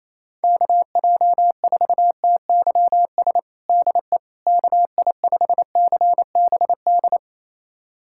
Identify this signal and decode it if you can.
Morse code, heard as a keyed tone, about 28 wpm, 700 Hz. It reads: KJ4TYS DE KI5CBD
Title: USA Callsign DE Callsign - 6 Characters - 28wpm